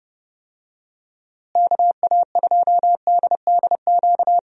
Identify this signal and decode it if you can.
Morse code, heard as a keyed tone, about 30 wpm, 700 Hz. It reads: KA2DDQ
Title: USA Call Signs - 6 Characters - Repeated 3x - 30wpm